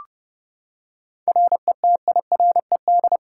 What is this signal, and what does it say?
Morse code, heard as a keyed tone, about 30 wpm, 700 Hz. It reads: RETIRED
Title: Practice QSO - Exchange 3 - More Info - 30wpm